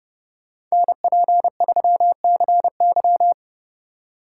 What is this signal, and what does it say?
Morse code, heard as a keyed tone, about 39 wpm, 700 Hz. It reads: NP3CY